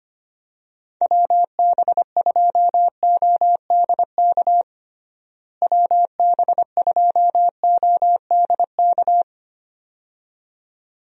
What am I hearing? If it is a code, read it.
Morse code, heard as a keyed tone, about 25 wpm, 700 Hz. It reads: WB2ODK WB2ODK
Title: USA Call Signs - 6 Characters - Repeated 2x - 25wpm